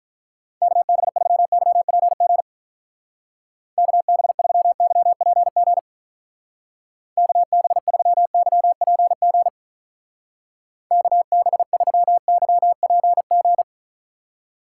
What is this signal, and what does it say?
Morse code, heard as a keyed tone, about 45 wpm, 700 Hz. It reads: KB3YPG KB3YPG KB3YPG KB3YPG